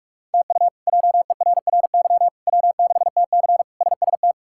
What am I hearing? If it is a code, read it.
Morse code, heard as a keyed tone, about 41 wpm, 700 Hz. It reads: TU JERRY W6TC SST
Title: K1USN SST QSO Practice - 45wpm